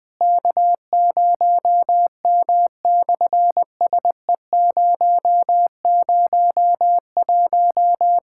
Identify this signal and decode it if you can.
Morse code, heard as a keyed tone, about 19 wpm, 700 Hz. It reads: K0M/SE001
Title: SOTA QSO Practice 20wpm